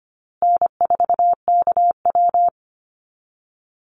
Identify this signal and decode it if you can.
Morse code, heard as a keyed tone, about 25 wpm, 700 Hz. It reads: N4KW
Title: Easy Call Signs Repeated 2x - 25wpm